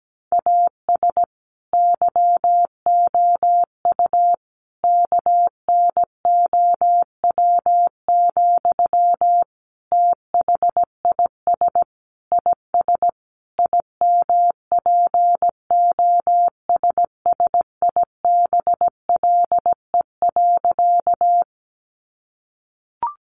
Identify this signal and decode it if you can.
Morse code, heard as a keyed tone, about 17 wpm, 700 Hz. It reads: AS YOU KNOW, THIS IS IMPOSSIBLE.
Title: Sentences from Top 1100 Words 17wpm